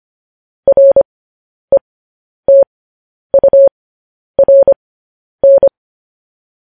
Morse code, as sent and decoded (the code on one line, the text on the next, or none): .-. - ..- .-. -.
RETURN